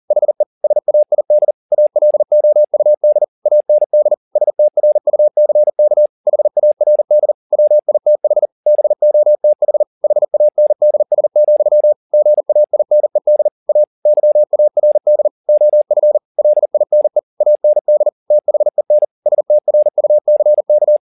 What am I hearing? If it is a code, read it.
Morse code, heard as a keyed tone, about 40 wpm, 600 Hz. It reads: HE SAID ALOUD AND STRUCK HARD WITH BOTH HANDS, GAINED A YARD OF LINE AND THEN STRUCK